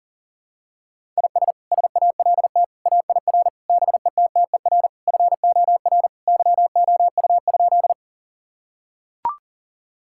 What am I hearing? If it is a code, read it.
Morse code, heard as a keyed tone, about 40 wpm, 700 Hz. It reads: IS SALT AIR BETTER FOR YOU?
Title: Sentences from Top 1000 Words - Review Part 5 - 40wpm